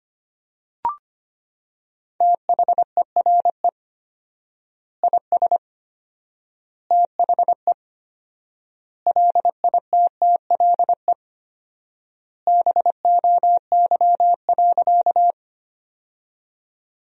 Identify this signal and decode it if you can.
Morse code, heard as a keyed tone, about 25 wpm, 700 Hz. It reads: THERE IS THE LITTLE BOY.
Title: 4x Word-Spacing - Sentences from Top 200 Words - 25wpm